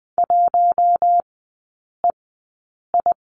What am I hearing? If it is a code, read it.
Morse code, heard as a keyed tone, about 20 wpm, spaced as for 8 wpm, 700 Hz. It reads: 1EI
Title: 041 Lesson 8e - Number 1 - Three Characters 20wpm